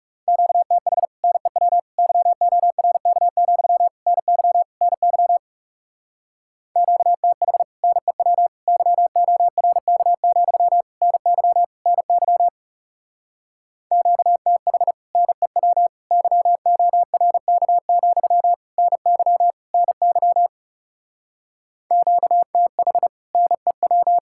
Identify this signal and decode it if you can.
Morse code, heard as a keyed tone, about 40 wpm, 700 Hz. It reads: QTH NEW YORK, NY NY QTH NEW YORK, NY NY QTH NEW YORK, NY NY QTH NEW